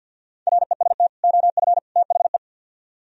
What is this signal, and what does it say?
Morse code, heard as a keyed tone, about 62 wpm, 700 Hz. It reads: REST OF THE
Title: Speed-Racing - Sets of 3 Words 50wpm